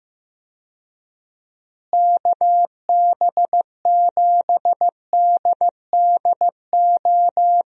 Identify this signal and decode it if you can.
Morse code, heard as a keyed tone, about 15 wpm, 700 Hz. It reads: KB7DDO
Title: USA Call Signs - 6 Characters - Repeated 2x - 15wpm